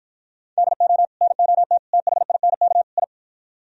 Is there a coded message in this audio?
Yes: Morse code from 0.6 to 3.0 s, heard as a keyed tone, about 53 wpm, 700 Hz. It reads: DO NOT THINK I